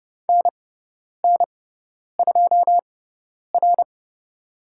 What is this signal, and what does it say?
Morse code, heard as a keyed tone, about 30 wpm, spaced as for 10 wpm, 700 Hz. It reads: NN2R
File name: Lesson 17g - Letter u - Call Signs 30wpm